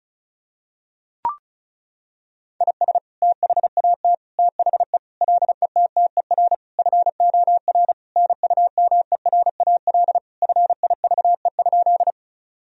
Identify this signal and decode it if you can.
Morse code, heard as a keyed tone, about 34 wpm, 700 Hz. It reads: IS THAT THE LETTER FOR NUMERAL FIVE?
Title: Sentences from Top 1000 Words - Review Part 2 - 35wpm